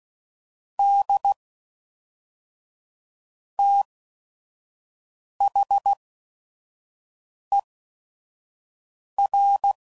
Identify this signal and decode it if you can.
Morse code, heard as a keyed tone, about 16 wpm, spaced as for 5 wpm, 800 Hz. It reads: D THER
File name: CW Code Groups @5WPM / Technical Subjects / AND Gate